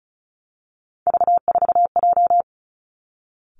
Morse code, heard as a keyed tone, about 36 wpm, 700 Hz. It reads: V4J
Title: Three Challenging Letter-Number Combinations 35wpm